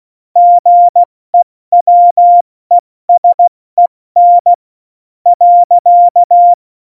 Morse code, heard as a keyed tone, about 16 wpm, spaced as for 13 wpm, 700 Hz. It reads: GEWESEN.